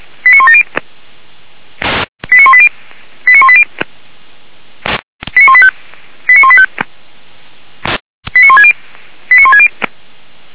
Five-tone selective call (ZVEI), heard as quick runs of five tones, tones of about 80 ms. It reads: 89170 89180 89180 89185 89185 89160 89160